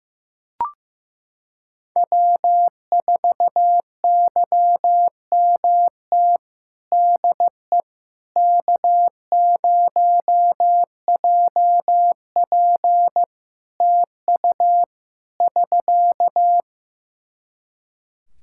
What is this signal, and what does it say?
Morse code, heard as a keyed tone, about 15 wpm, 700 Hz. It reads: W4YMT DE K0JP TU <SK>
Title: Practice QSO - Exchange 4 - Ending - 15wpm